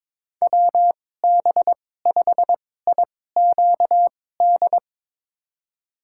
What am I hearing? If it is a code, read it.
Morse code, heard as a keyed tone, about 22 wpm, spaced as for 16 wpm, 700 Hz. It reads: WB5IQD
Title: USA Call Signs - 6 Characters - 2x Character Spacing - 22wpm